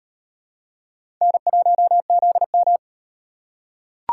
Morse code, heard as a keyed tone, about 38 wpm, 700 Hz. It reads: N1ZM